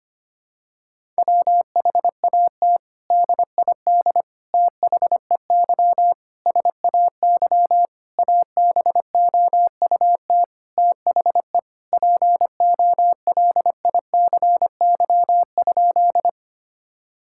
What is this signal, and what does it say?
Morse code, heard as a keyed tone, about 25 wpm, 700 Hz. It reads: WHAT DID THEY SAY ABOUT THE POLICY?
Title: Sentences from Top 1000 Words - Review Part 5 - 25wpm